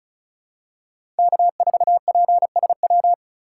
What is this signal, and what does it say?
Morse code, heard as a keyed tone, about 34 wpm, 700 Hz. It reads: K4PSW
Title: USA Call Signs - 5 Characters - 4x Character Spacing - 35wpm